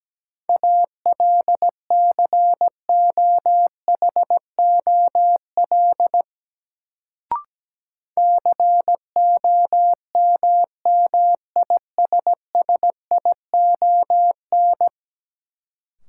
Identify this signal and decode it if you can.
Morse code, heard as a keyed tone, about 17 wpm, 700 Hz. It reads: ALCOHOL COMMISSION
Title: Sentences from Top 1400 Words - New Word Drill - 17wpm